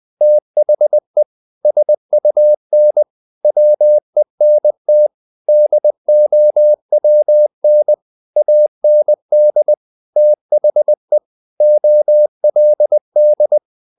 Morse code, heard as a keyed tone, about 20 wpm, 600 Hz. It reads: THE SUN WENT DOWN AND THE OLD
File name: the old man and the sea / the old man and the sea part 1 20wpm